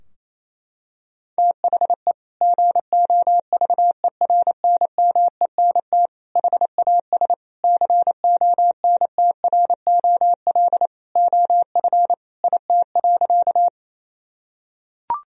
Morse code, heard as a keyed tone, about 28 wpm, 700 Hz. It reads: THE GOVERNMENT HAS CONTROL OF IT.
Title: Sentences from Top 1000 Words - Review Part 3 - 28wpm